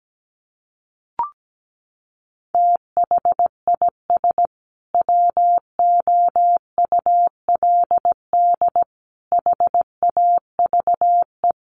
Morse code, heard as a keyed tone, about 17 wpm, 700 Hz. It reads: THIS WOULD HAVE
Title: Sentences from Top 800 Words 17wpm